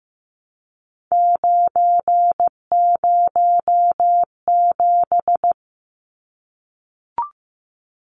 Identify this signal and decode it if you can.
Morse code, heard as a keyed tone, about 15 wpm, 700 Hz. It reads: 907
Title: Numbers - 3-Digit 15wpm